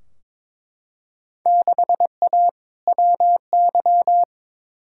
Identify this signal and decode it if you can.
Morse code, heard as a keyed tone, about 22 wpm, 700 Hz. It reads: 6A WY
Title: Field Day Exchanges Encore - 22wpm